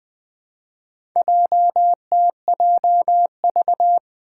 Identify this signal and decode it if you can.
Morse code, heard as a keyed tone, about 20 wpm, 700 Hz. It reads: JTJV
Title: Four Challenging Letter-Number Combinations 20wpm